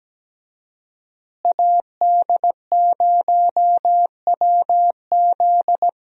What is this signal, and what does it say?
Morse code, heard as a keyed tone, about 17 wpm, 700 Hz. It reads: AD0WZ
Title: USA Call Signs - 5 Characters - Repeated 2x - 17wpm